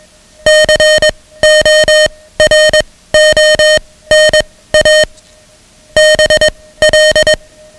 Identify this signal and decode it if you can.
Morse code, heard as a keyed tone, about 21 wpm, spaced as for 16 wpm, 620 Hz, with noise. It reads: CORONA BL
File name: ham radio / CodeQuick / Code Quick 10